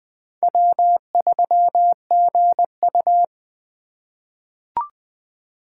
Morse code, heard as a keyed tone, about 21 wpm, 700 Hz. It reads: W3GU
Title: USA Call Signs - 4 Characters - 8x Character Spacing - 20wpm